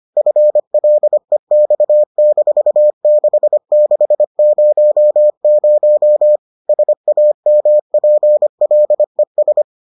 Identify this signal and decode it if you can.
Morse code, heard as a keyed tone, about 25 wpm, 600 Hz. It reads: FLEX-6600 SAMPLES